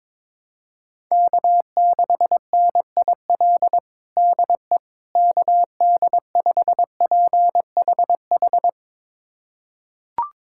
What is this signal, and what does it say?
Morse code, heard as a keyed tone, about 22 wpm, 700 Hz. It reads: K6NIL DE KD5PHH
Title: USA Callsign DE Callsign 22wpm